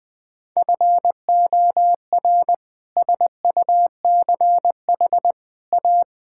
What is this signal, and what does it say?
Morse code, heard as a keyed tone, about 20 wpm, 700 Hz. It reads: FOR SUCH A
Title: Sentences from Top 1000 Words - Review Part 6 - 20wpm